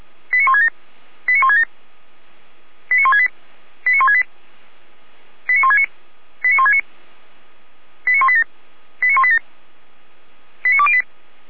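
Five-tone selective call (ZVEI), heard as quick runs of five tones, tones of about 80 ms. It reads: 89167 89167 89168 89168 89169 89169 89187 89187 89208
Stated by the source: ZVEI five-tone call 89130 received on SDR.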